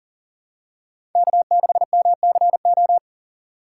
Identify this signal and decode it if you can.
Morse code, heard as a keyed tone, about 40 wpm, 700 Hz. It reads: K6MCO
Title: USA Call Signs - 5 Characters - 4x Character Spacing - 40wpm